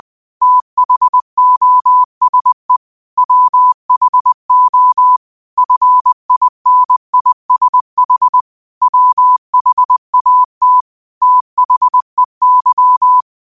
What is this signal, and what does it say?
Morse code, heard as a keyed tone, about 20 wpm, 1000 Hz. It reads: THOSE WHO FINISH WHAT THEY